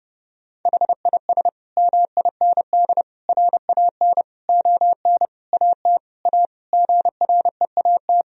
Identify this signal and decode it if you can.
Morse code, heard as a keyed tone, about 30 wpm, 700 Hz. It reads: HIS MIND RAN ON AT A GREAT
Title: Sentences from Top 1000 Words - Review Part 4 - 30wpm